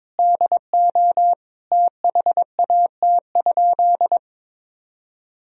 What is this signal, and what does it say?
Morse code, heard as a keyed tone, about 21 wpm, 700 Hz. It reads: DO THAT?
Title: Sentences from Top 1000 Words - Review Part 3 - 22wpm